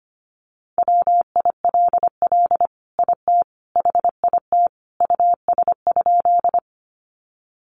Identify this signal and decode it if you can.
Morse code, heard as a keyed tone, about 25 wpm, 700 Hz. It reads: WILL IT HIT US?
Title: Sentences from Top 1200 Words 25wpm